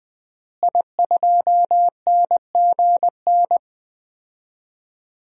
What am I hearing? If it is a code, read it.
Morse code, heard as a keyed tone, about 20 wpm, 700 Hz. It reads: I2NGN
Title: DXCC-100 Call Signs - 5 Characters - Repeated 2x - 20wpm